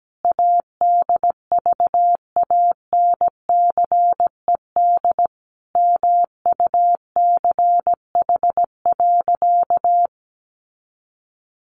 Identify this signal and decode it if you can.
Morse code, heard as a keyed tone, about 17 wpm, 700 Hz. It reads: ADVANCED MUCH.
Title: Sentences from Top 1200 Words 17wpm